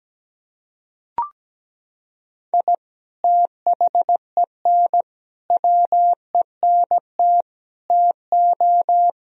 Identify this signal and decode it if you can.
Morse code, heard as a keyed tone, about 17 wpm, 700 Hz. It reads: I THEN WENT TO
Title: Sentences from Top 1000 Words - Review Part 3 - 17wpm